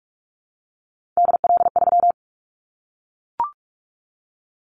Morse code, heard as a keyed tone, about 44 wpm, 700 Hz. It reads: DL3
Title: Callsign Prefixes for Top 25 DXCC - Repeated 3x - 45wpm